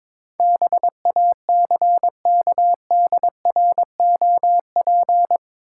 Morse code, heard as a keyed tone, about 22 wpm, 700 Hz. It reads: BACKDROP
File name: practice / 22wpm / Eight Letter Words - 22wpm